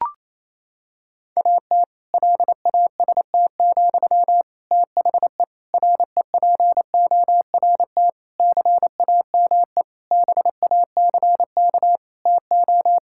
Morse code, heard as a keyed tone, about 28 wpm, 700 Hz. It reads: AT LAST, THE REPORT CAME BACK TO
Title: Sentences from Top 1000 Words - Review Part 6 - 28wpm